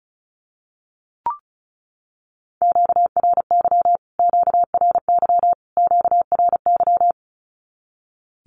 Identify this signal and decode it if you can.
Morse code, heard as a keyed tone, about 34 wpm, 700 Hz. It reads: QRY QRY QRY
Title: Q-Codes Repeated 3x 35wpm